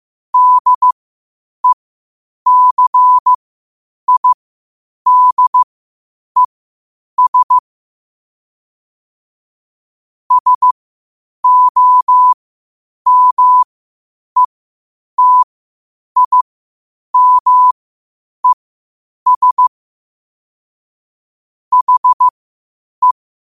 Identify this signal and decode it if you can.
Morse code, heard as a keyed tone, about 15 wpm, spaced as for 8 wpm, 1000 Hz. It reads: DECIDES SOMETIMES HE
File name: qotdfiles / 8wpm